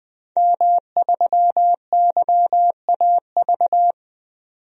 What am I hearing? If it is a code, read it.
Morse code, heard as a keyed tone, about 20 wpm, 700 Hz. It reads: M3YAV
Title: DXCC-100 Call Signs - 5 Characters - 20wpm